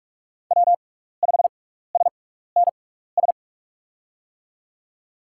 Morse code, heard as a keyed tone, about 44 wpm, spaced as for 15 wpm, 690 Hz. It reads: W5SNS
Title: USA Call Signs - 5 Characters - 6x Character Spacing - 45wpm